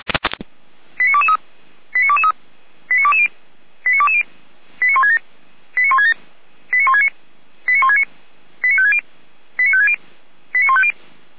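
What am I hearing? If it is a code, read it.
Five-tone selective call (ZVEI), heard as quick runs of five tones, tones of about 70 ms. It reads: 89223 89223 89220 89220 89167 89167 89169 89169 89560 89560 89150
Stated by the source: ZVEI five-tone call received on SDR.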